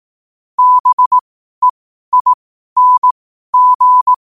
Morse code, heard as a keyed tone, about 18 wpm, spaced as for 12 wpm, 1000 Hz. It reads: BEING
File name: qotdfiles / 12wpm